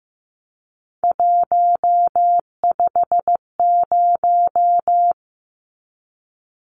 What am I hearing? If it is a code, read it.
Morse code, heard as a keyed tone, about 15 wpm, 700 Hz. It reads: 150
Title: Numbers - 3-Digit 15wpm